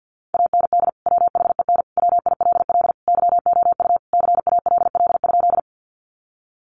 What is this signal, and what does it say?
Morse code, heard as a keyed tone, about 50 wpm, 700 Hz. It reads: AND WHEN WILL YOU CALL?
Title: Sentences from Top 100 Words 50wpm